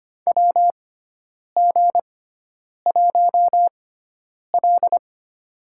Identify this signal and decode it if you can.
Morse code, heard as a keyed tone, about 25 wpm, spaced as for 9 wpm, 700 Hz. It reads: WG1L
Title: USA Call Signs - 6x Character Spacing - 25wpm